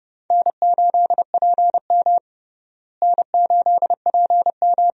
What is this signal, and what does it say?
Morse code, heard as a keyed tone, about 30 wpm, 700 Hz. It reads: N8PM N8PM
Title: USA Call Signs - 4 Characters - Repeated 2x - 30wpm